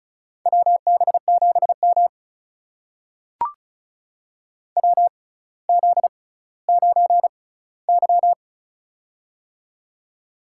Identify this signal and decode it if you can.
Morse code, heard as a keyed tone, about 36 wpm, spaced as for 18 wpm, 700 Hz. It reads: WB7M WZ9Y
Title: USA Call Signs - 4 Characters - 6x Character Spacing - 35wpm